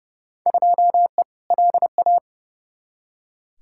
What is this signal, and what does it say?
Morse code, heard as a keyed tone, about 30 wpm, 700 Hz. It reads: 2E LA
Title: Field Day Exchanges Encore - 30wpm